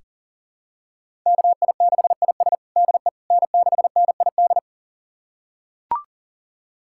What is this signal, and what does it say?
Morse code, heard as a keyed tone, about 40 wpm, 700 Hz. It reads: KI6IS DE N6NID
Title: USA Callsign DE Callsign - 5 Characters - 40wpm